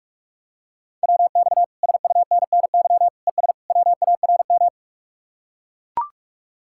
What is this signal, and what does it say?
Morse code, heard as a keyed tone, about 44 wpm, 700 Hz. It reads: WX SUNNY ES WARM